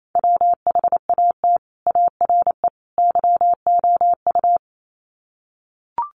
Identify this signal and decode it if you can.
Morse code, heard as a keyed tone, about 28 wpm, 700 Hz. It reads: WHAT ARE YOU
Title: Sets of 3 Words 28wpm